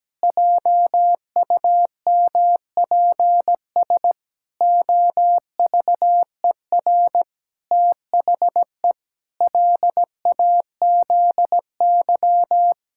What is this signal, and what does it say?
Morse code, heard as a keyed tone, about 17 wpm, 700 Hz. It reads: JUMPS OVER THE LAZY
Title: Pangram Challenge v2 - 17wpm